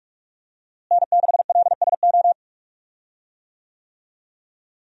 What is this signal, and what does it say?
Morse code, heard as a keyed tone, about 42 wpm, 700 Hz. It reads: N6LSO
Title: USA Call Signs - 5 Characters - Repeated 2x - 45wpm